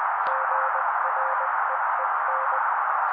Morse code, heard as a keyed tone, about 20 wpm, 530 Hz. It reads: GREEN
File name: daily / 18wpm noise